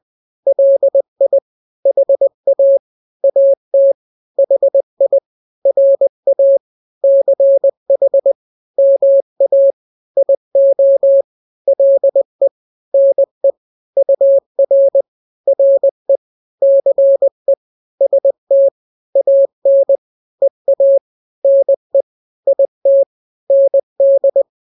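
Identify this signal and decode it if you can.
Morse code, heard as a keyed tone, about 20 wpm, 550 Hz. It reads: LI HA AT HI RA CH MA IO LE NE UR RE CE ST AN EA NE IT ND
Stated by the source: Morse drill of two-letter n-grams at 20/18 WPM (Farnsworth), from KK7RR.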